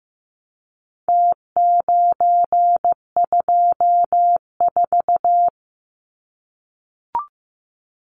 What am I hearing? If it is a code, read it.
Morse code, heard as a keyed tone, about 15 wpm, 700 Hz. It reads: T924
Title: Numbers - Contest Style - 4-Digit 15wpm